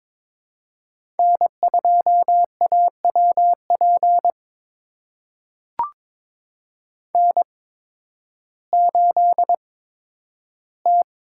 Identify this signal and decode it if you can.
Morse code, heard as a keyed tone, about 22 wpm, spaced as for 12 wpm, 700 Hz. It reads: N2AWP N8T
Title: USA Call Signs - 5 Characters - 8x Character Spacing - 22wpm